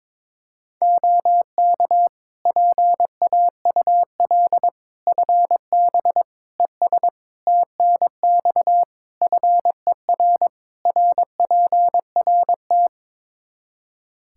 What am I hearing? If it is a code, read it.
Morse code, heard as a keyed tone, about 22 wpm, 700 Hz. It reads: OK PAUL FB ES TNX FER RPRT
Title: Practice QSO - Exchange 2 - Info - 22wpm